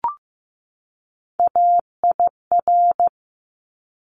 Morse code, heard as a keyed tone, about 15 wpm, 700 Hz. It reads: AIR
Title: Top 200 Words 15wpm